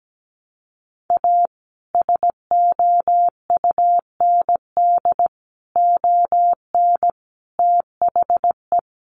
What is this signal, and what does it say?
Morse code, heard as a keyed tone, about 17 wpm, 700 Hz. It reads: A SOUND ON THE
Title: A Sound Of Thunder 17wpm